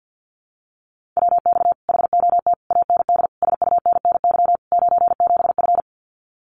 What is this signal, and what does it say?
Morse code, heard as a keyed tone, about 50 wpm, 700 Hz. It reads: WX HOT AND SUNNY 97F
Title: Common QSO Phrases Encore 50wpm